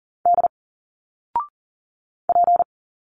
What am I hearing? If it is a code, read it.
Morse code, heard as a keyed tone, about 40 wpm, 700 Hz. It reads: D P
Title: Single Letters 40wpm